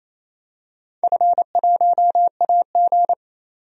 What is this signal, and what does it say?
Morse code, heard as a keyed tone, about 28 wpm, 700 Hz. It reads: F1AG
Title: DXCC-100 Call Signs - 4 Characters - 28wpm